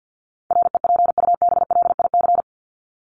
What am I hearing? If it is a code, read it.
Morse code, heard as a keyed tone, about 50 wpm, 700 Hz. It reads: REPUBLIC
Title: Top 1251-1500 Words 50wpm